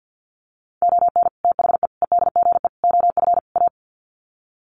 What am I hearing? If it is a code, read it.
Morse code, heard as a keyed tone, about 49 wpm, 700 Hz. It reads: ON THE EDGE OF A
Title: Sets of 5 Words Encore2 50wpm